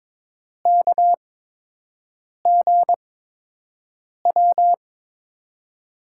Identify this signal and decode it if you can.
Morse code, heard as a keyed tone, about 22 wpm, spaced as for 6 wpm, 700 Hz. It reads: KGW
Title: USA Call Signs - 6 Characters - 8x Character Spacing - 22wpm